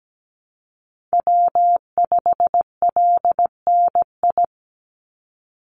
Morse code, heard as a keyed tone, about 17 wpm, 700 Hz. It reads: W5LNI
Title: Call Signs - 17wpm